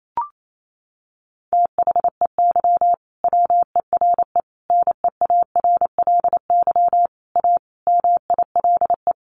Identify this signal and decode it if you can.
Morse code, heard as a keyed tone, about 28 wpm, 700 Hz. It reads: THEY WERE NEARLY A MILE
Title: Sentences from Top 1300 Words 28wpm